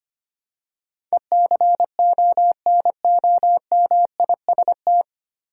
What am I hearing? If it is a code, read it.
Morse code, heard as a keyed tone, about 25 wpm, 700 Hz. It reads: ECONOMIST
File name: Nine Letter Words - 25wpm